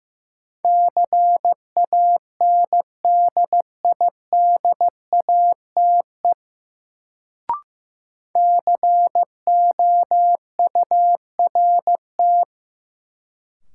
Morse code, heard as a keyed tone, about 15 wpm, 700 Hz. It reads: CANDIDATE COURT
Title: Sentences from Top 1600 Words - New Word Drill - 15wpm